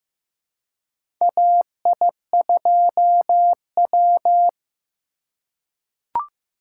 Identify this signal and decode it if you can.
Morse code, heard as a keyed tone, about 15 wpm, 700 Hz. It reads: AI2W
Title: USA Call Signs - 4 Characters - 4x Character Spacing - 15wpm